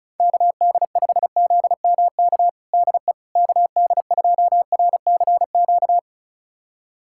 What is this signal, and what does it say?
Morse code, heard as a keyed tone, about 36 wpm, 700 Hz. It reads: KD5ZMK DE KD2RCQ
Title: USA Callsign DE Callsign - 6 Characters - 35wpm